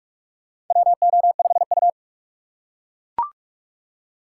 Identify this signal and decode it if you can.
Morse code, heard as a keyed tone, about 42 wpm, 700 Hz. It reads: WO5U
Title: USA Call Signs - 4 Characters - 4x Character Spacing - 45wpm